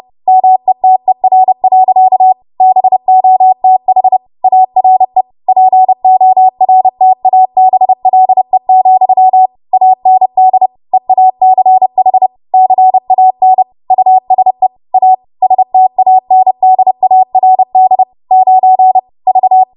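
Morse code, heard as a keyed tone, about 30 wpm, 750 Hz. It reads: METER. BOTH ARE PORTABLE, AND EACH CAN USE A STANDARD 9 V
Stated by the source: W1AW Web Code Practice Files, 30 WPM run from ARRL; text from April 2023 QST.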